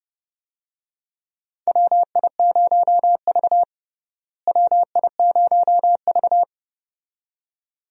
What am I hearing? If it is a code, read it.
Morse code, heard as a keyed tone, about 32 wpm, 700 Hz. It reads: WI0V WI0V
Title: USA Call Signs - 4 Characters - Repeated 2x - 30wpm